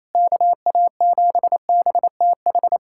Morse code, heard as a keyed tone, about 27 wpm, 700 Hz. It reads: KA7BTH
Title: USA Callsign DE Callsign 28wpm